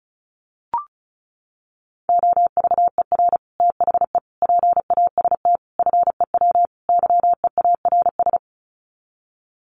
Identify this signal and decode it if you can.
Morse code, heard as a keyed tone, about 34 wpm, 700 Hz. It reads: OVER THE PAST FEW YEARS
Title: Sets of 5 Words Encore 35wpm